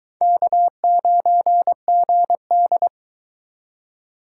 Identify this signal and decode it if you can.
Morse code, heard as a keyed tone, about 24 wpm, 700 Hz. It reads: K9GD